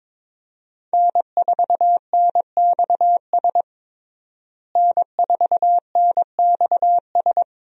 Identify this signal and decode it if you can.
Morse code, heard as a keyed tone, about 22 wpm, 700 Hz. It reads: N4NXS N4NXS
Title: USA Call Signs - 5 Characters - Repeated 2x - 22wpm